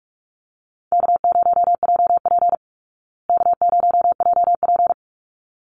Morse code, heard as a keyed tone, about 44 wpm, 700 Hz. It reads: K0JP K0JP
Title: Call Signs Repeated 3x - 45wpm